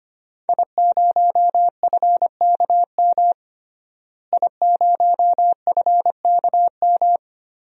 Morse code, heard as a keyed tone, about 25 wpm, 700 Hz. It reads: I0FKM I0FKM
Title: DXCC-100 Call Signs - 5 Characters - Repeated 3x - 25wpm